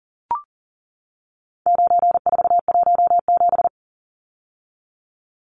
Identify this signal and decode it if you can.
Morse code, heard as a keyed tone, about 40 wpm, 700 Hz. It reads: 9417
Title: Numbers - 4-Digit 40wpm